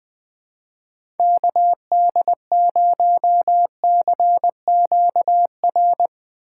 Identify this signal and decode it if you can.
Morse code, heard as a keyed tone, about 20 wpm, 700 Hz. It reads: KD0CQR
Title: USA Call Signs - 6 Characters - 4x Character Spacing - 20wpm